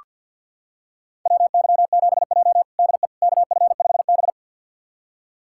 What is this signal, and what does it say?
Morse code, heard as a keyed tone, about 48 wpm, 700 Hz. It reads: WY7J DE KF5B
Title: USA Callsign DE Callsign - 4 Characters - 50wpm